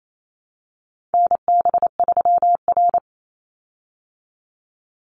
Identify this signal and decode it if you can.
Morse code, heard as a keyed tone, about 28 wpm, 700 Hz. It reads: NB3R
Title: Call Signs Repeated 3x - 28wpm